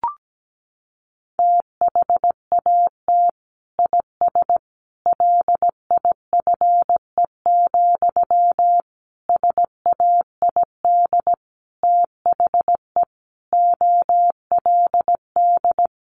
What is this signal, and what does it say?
Morse code, heard as a keyed tone, about 17 wpm, 700 Hz. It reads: THAT IS LIFE, SAID THE OLD
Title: Sentences from Top 900 Words 17wpm